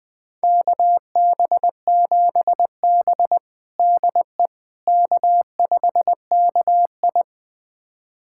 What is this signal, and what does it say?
Morse code, heard as a keyed tone, about 20 wpm, 700 Hz. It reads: KB7B DE K5KI
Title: USA Callsign DE Callsign - 4 Characters - 20wpm